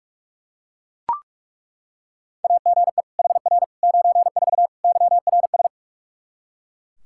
Morse code, heard as a keyed tone, about 41 wpm, 700 Hz. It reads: AGE HR 94 YRS
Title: K1USN SST QSO Practice 45wpm